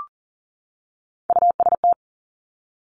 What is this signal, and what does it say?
Morse code, heard as a keyed tone, about 41 wpm, 690 Hz. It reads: UST
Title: Top Three Letter Combinations in Words 40wpm